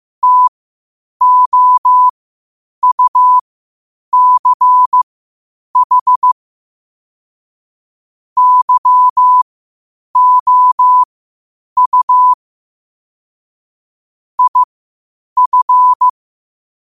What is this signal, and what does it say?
Morse code, heard as a keyed tone, about 15 wpm, spaced as for 8 wpm, 1000 Hz. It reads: TOUCH YOU IF
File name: qotdfiles / 8wpm